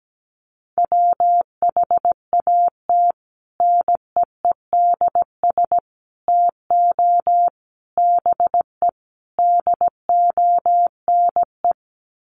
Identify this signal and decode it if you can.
Morse code, heard as a keyed tone, about 17 wpm, 700 Hz. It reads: WHAT NEEDS TO BE DONE
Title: Sets of 5 Words Encore 17wpm